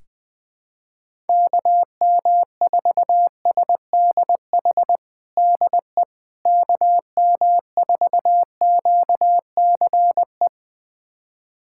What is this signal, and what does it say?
Morse code, heard as a keyed tone, about 20 wpm, 700 Hz. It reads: KM4SDH DE KM4QCE
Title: USA Callsign DE Callsign 20wpm